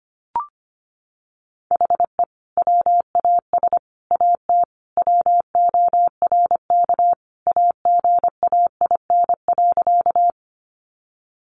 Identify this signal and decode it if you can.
Morse code, heard as a keyed tone, about 25 wpm, 700 Hz. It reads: HE WAS AT WORK AGAIN.
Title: Sentences from Top 200 Words 25wpm